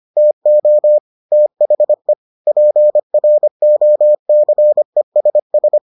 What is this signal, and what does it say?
Morse code, heard as a keyed tone, about 25 wpm, 600 Hz. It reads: TO THE PROCESS